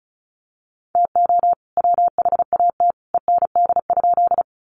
Morse code, heard as a keyed tone, about 34 wpm, 700 Hz. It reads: TO WHAT END?
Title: Sentences from Top 200 Words 35wpm